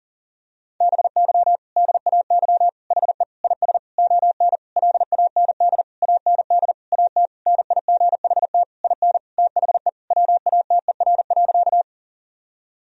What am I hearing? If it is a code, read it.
Morse code, heard as a keyed tone, about 40 wpm, 700 Hz. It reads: BY DAY HE IS ON LAND AND AT NIGHT IN THE WATER.
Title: Sentences from Top 1000 Words - Review Part 1 - 40wpm